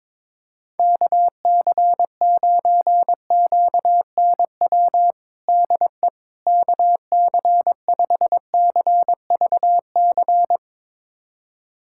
Morse code, heard as a keyed tone, about 21 wpm, 700 Hz. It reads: KC9QNW DE KC5CVC
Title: USA Callsign DE Callsign 22wpm